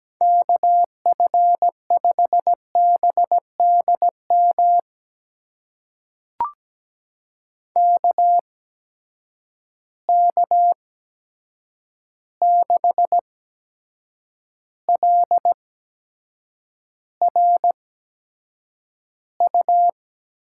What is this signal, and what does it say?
Morse code, heard as a keyed tone, about 17 wpm, spaced as for 7 wpm, 700 Hz. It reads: KF5BDM KK6LRU
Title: USA Call Signs - 6 Characters - 8x Character Spacing - 17wpm